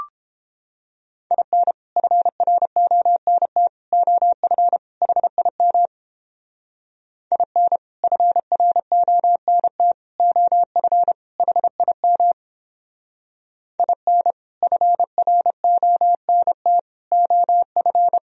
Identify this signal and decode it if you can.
Morse code, heard as a keyed tone, about 29 wpm, 700 Hz. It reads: IN FRONT OF HIM IN FRONT OF HIM IN FRONT OF